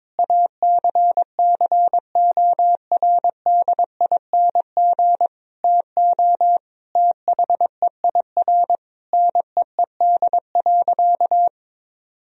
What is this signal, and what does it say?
Morse code, heard as a keyed tone, about 22 wpm, 700 Hz. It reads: ACCORDING TO THEIR NEED.
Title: Sentences from Top 1000 Words - Review Part 4 - 22wpm